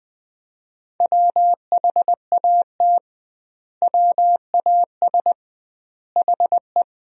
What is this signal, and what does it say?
Morse code, heard as a keyed tone, about 20 wpm, 700 Hz. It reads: WHAT WAS HE
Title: 2x Word-Spacing - Sentences from Top 100 Words - 20wpm